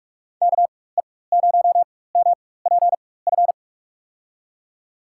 Morse code, heard as a keyed tone, about 44 wpm, spaced as for 21 wpm, 700 Hz. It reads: KE0MPF
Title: USA Call Signs - 6 Characters - 4x Character Spacing - 45wpm